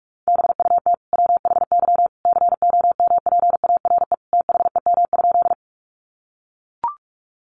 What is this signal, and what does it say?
Morse code, heard as a keyed tone, about 44 wpm, 700 Hz. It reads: BUT WHY COMPARE THEM?
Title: Sentences from Top 1300 Words 45wpm